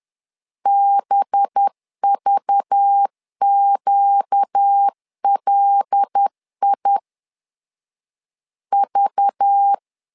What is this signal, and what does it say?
Morse code, heard as a keyed tone, about 10 wpm, 800 Hz. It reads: BVQLI V
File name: LEZIONI / 5 LBVC / 5 LBV 10wpm 2248